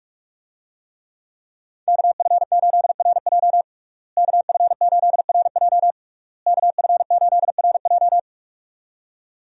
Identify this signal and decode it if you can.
Morse code, heard as a keyed tone, about 43 wpm, 700 Hz. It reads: KF8RJ KF8RJ KF8RJ
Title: USA Call Signs - 5 Characters - Repeated 3x - 45wpm